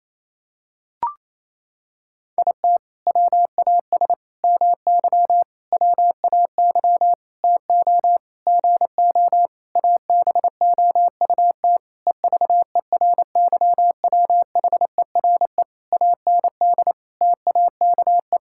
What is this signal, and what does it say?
Morse code, heard as a keyed tone, about 28 wpm, 700 Hz. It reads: IT WAS MY WAY TO GO ABOUT EVERYWHERE AND TAKE
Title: Sentences from Top 1600 Words 28wpm